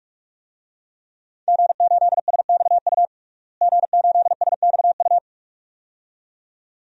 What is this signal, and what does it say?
Morse code, heard as a keyed tone, about 46 wpm, 700 Hz. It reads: G8SXU G8SXU
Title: DXCC-100 Call Signs - 5 Characters - Repeated 2x - 45wpm